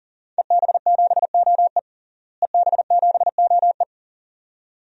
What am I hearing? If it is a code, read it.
Morse code, heard as a keyed tone, about 40 wpm, 700 Hz. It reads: EB7OE EB7OE
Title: DXCC-100 Call Signs - 5 Characters - Repeated 2x - 40wpm